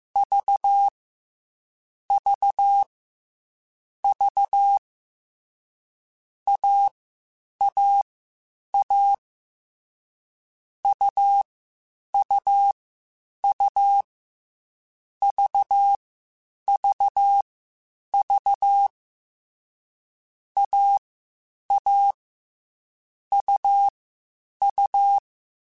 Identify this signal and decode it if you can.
Morse code, heard as a keyed tone, about 15 wpm, spaced as for 8 wpm, 800 Hz. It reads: VVV AAA UUU VVV AA UU
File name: Difficult Combos / Difficult Combos-mp3s / AUV ShortRun @13